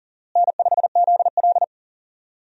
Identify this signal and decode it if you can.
Morse code, heard as a keyed tone, about 40 wpm, 700 Hz. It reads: N5ZL
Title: USA Call Signs - 4 Characters - Repeated 2x - 40wpm